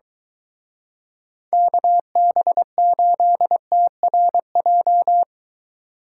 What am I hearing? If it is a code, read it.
Morse code, heard as a keyed tone, about 23 wpm, 700 Hz. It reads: KB8TRJ